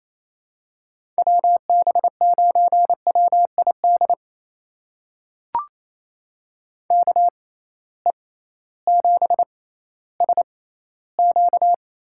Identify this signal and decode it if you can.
Morse code, heard as a keyed tone, about 28 wpm, spaced as for 15 wpm, 700 Hz. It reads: WB9WID KE7SQ
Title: USA Call Signs - 6 Characters - 6x Character Spacing - 28wpm